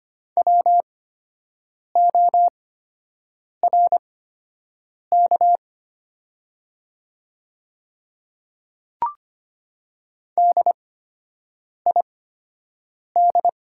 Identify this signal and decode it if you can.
Morse code, heard as a keyed tone, about 25 wpm, spaced as for 7 wpm, 700 Hz. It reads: WORK DID
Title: 8x Character Spacing - Top 200 Words - 25wpm